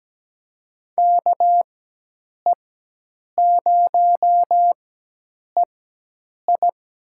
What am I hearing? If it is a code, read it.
Morse code, heard as a keyed tone, about 17 wpm, spaced as for 8 wpm, 700 Hz. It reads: KE0EI
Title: USA Call Signs - 4x Character Spacing - 17wpm